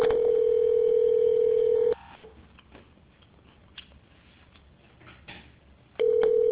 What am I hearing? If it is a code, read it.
Ringback tone (North America), 440 + 480 Hz.